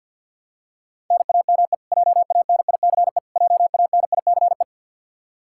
Morse code, heard as a keyed tone, about 53 wpm, 700 Hz. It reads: NAME JANICE JANICE
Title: Practice QSO - Exchange 1 - Beginning - 50wpm